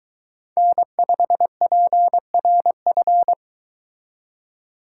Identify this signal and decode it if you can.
Morse code, heard as a keyed tone, about 22 wpm, 700 Hz. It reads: N5PRF